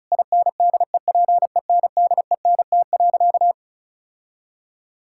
Morse code, heard as a keyed tone, about 36 wpm, 700 Hz. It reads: INDEPENDENT.
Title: Sentences from Top 1000 Words - Review Part 6 - 35wpm